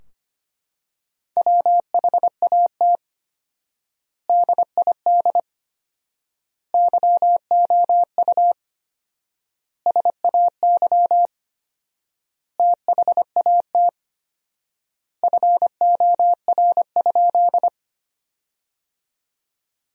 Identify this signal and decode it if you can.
Morse code, heard as a keyed tone, about 25 wpm, 700 Hz. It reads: WHAT DID YOU SAY THAT FOR?
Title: 4x Word-Spacing - Sentences from Top 200 Words - 25wpm